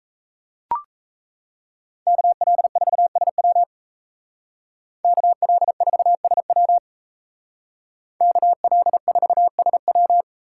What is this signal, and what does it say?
Morse code, heard as a keyed tone, about 36 wpm, 700 Hz. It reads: KL4SW KL4SW KL4SW